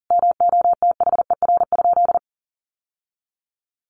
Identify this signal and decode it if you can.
Morse code, heard as a keyed tone, about 37 wpm, 700 Hz. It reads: MOTHER?
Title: Sentences from Top 1000 Words 40wpm